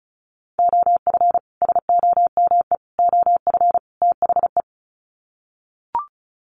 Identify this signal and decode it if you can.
Morse code, heard as a keyed tone, about 34 wpm, 700 Hz. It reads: OF SOME OF THE
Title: Sets of 4 Words 35wpm